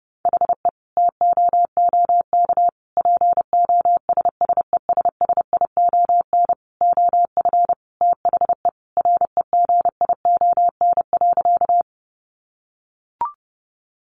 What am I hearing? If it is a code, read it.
Morse code, heard as a keyed tone, about 30 wpm, 700 Hz. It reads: HE TOOK POSSESSION OF THE REGION.
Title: Sentences from Top 1500 Words 30wpm